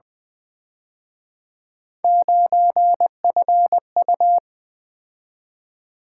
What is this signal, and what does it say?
Morse code, heard as a keyed tone, about 24 wpm, 700 Hz. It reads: 9FU
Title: Speed-Racing - Three Letter-Number Combinations - 20wpm